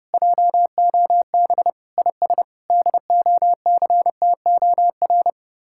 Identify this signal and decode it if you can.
Morse code, heard as a keyed tone, about 32 wpm, 700 Hz. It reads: JOB IS DOCTOR